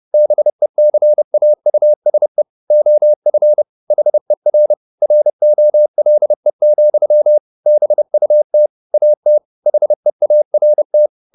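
Morse code, heard as a keyed tone, about 30 wpm, 600 Hz. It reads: BECAUSE OF HER ROLE, BUT AT HEART
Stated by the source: Morse code at 30 wpm, the novel Never by Ken Follett in English, from CW med Gnister.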